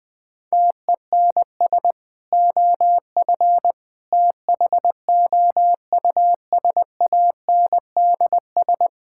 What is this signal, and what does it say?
Morse code, heard as a keyed tone, about 20 wpm, 700 Hz. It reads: TENS OF THOUSANDS
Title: Sentences with 4-Word Phrases - 20wpm